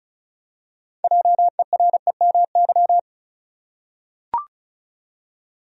Morse code, heard as a keyed tone, about 34 wpm, 700 Hz. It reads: JEREMY
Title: USA Common Names - 35wpm